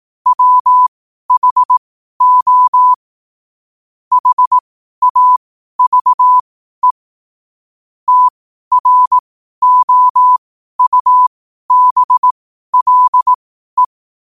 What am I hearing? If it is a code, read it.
Morse code, heard as a keyed tone, about 18 wpm, spaced as for 12 wpm, 1000 Hz. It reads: WHO HAVE TROUBLE
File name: qotdfiles / 12wpm